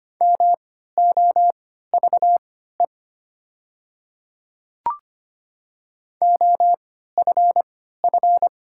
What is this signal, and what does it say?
Morse code, heard as a keyed tone, about 25 wpm, spaced as for 14 wpm, 700 Hz. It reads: MOVE OFF
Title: 3x Character Spacing - Top 200 Words - 25wpm